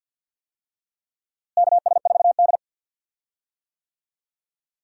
Morse code, heard as a keyed tone, about 51 wpm, 700 Hz. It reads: KS4D
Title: USA Call Signs - 4 Characters - Repeated 3x - 50wpm